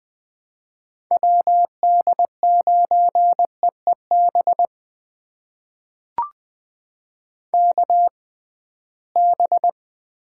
Morse code, heard as a keyed tone, about 20 wpm, spaced as for 15 wpm, 700 Hz. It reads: WD9EEB KB
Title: USA Call Signs - 6 Characters - 6x Character Spacing - 20wpm